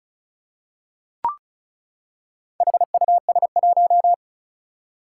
Morse code, heard as a keyed tone, about 36 wpm, 700 Hz. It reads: HUS1
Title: Four Challenging Letter-Number Combinations 35wpm